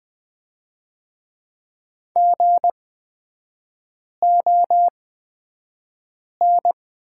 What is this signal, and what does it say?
Morse code, heard as a keyed tone, about 20 wpm, spaced as for 5 wpm, 700 Hz. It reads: GON
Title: 224 Lesson 36e - BT - Three Characters 20wpm